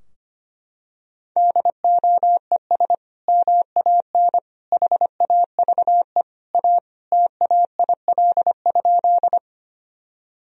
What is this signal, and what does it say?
Morse code, heard as a keyed tone, about 25 wpm, 700 Hz. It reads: DOES MAN HAVE A TAIL?
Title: Sentences from Top 1000 Words - Review Part 3 - 25wpm